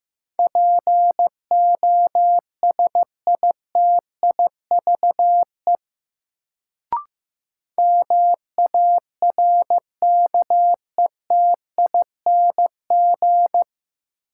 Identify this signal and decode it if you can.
Morse code, heard as a keyed tone, about 15 wpm, 700 Hz. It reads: POSITIVE MARKETING
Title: Sentences from Top 1000 Words - Review Part 5 - New Word Drill - 15wpm